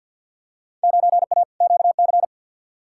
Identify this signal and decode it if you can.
Morse code, heard as a keyed tone, about 50 wpm, 700 Hz. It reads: 9A QC
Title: Field Day Exchanges Encore - 50wpm